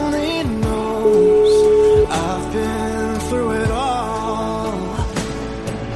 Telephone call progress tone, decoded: ringback tone

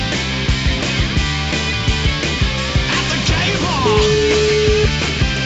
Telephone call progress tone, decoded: ringback tone

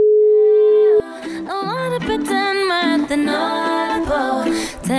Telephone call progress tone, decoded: ringback tone

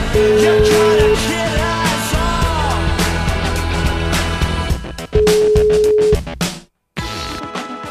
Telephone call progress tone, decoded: ringback tone

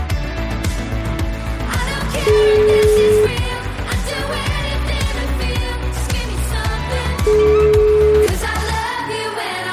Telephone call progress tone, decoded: ringback tone

